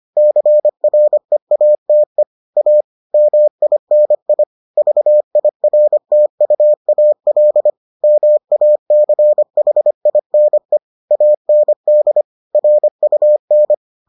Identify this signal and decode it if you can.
Morse code, heard as a keyed tone, about 25 wpm, 600 Hz. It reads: CREATE A MINI VIRTUAL MACHINE AND RUN